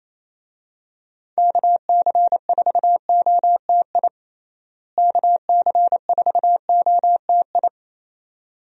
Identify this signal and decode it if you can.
Morse code, heard as a keyed tone, about 27 wpm, 700 Hz. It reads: KC4OTI KC4OTI
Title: USA Call Signs - 6 Characters - Repeated 2x - 28wpm